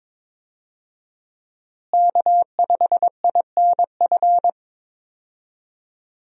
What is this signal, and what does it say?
Morse code, heard as a keyed tone, about 23 wpm, 700 Hz. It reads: K5INF
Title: USA Call Signs - 5 Characters - Repeated 3x - 22wpm